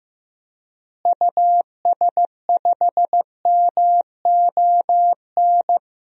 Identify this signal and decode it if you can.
Morse code, heard as a keyed tone, about 14 wpm, 700 Hz. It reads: US5MON
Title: DXCC-100 Call Signs - 6 Characters - Repeated 3x - 15wpm